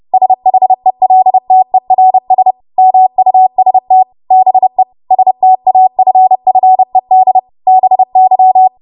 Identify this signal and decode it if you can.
Morse code, heard as a keyed tone, about 30 wpm, 750 Hz. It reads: SHELTERS MUST BE STAFFED BY
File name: Morse / 30WPM